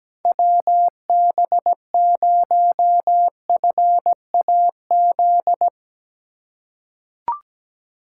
Morse code, heard as a keyed tone, about 17 wpm, 700 Hz. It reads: WB0FAZ
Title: USA Call Signs - 6 Characters - 3x Character Spacing - 17wpm